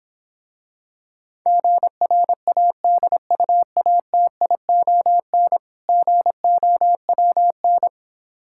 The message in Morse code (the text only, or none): GRADUATION GOWN